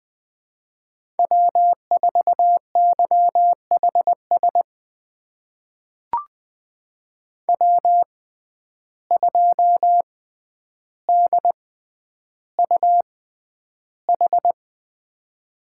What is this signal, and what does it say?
Morse code, heard as a keyed tone, about 20 wpm, spaced as for 10 wpm, 700 Hz. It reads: W4YHS W2DUH
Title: USA Call Signs - 5 Characters - 6x Character Spacing - 20wpm